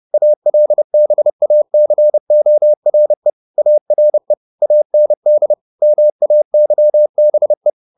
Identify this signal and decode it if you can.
Morse code, heard as a keyed tone, about 30 wpm, 600 Hz. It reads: ALBACORE ARE AND MAYBE
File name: the old man and the sea / the old man and the sea 1 30wpm